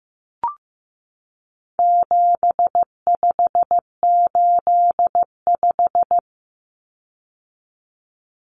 Morse code, heard as a keyed tone, about 15 wpm, 700 Hz. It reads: 7585
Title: Common Cut Numbers - 4-Digit 15wpm